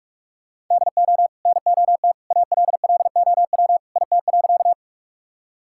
Morse code, heard as a keyed tone, about 46 wpm, 700 Hz. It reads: DO NOT ALLOW IT.
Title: Sentences from Top 1000 Words - Review Part 5 - 45wpm